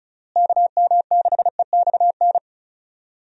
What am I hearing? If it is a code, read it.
Morse code, heard as a keyed tone, about 34 wpm, 700 Hz. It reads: KM6EXN
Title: USA Call Signs - 6 Characters - Repeated 2x - 35wpm